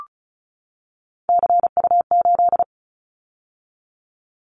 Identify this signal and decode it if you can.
Morse code, heard as a keyed tone, about 34 wpm, 700 Hz. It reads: CU8
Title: Callsign Prefixes for Top 50 DXCC - 35wpm